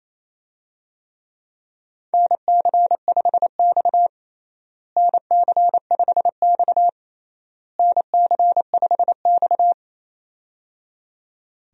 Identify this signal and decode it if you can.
Morse code, heard as a keyed tone, about 28 wpm, 700 Hz. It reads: NC5X NC5X NC5X
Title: USA Call Signs - 4 Characters - Repeated 3x - 28wpm